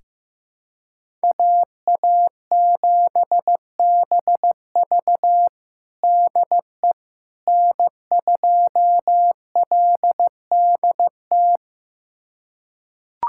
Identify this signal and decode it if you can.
Morse code, heard as a keyed tone, about 16 wpm, 700 Hz. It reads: AA7BV DE N2LDT
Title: USA Callsign DE Callsign - 5 Characters - 15wpm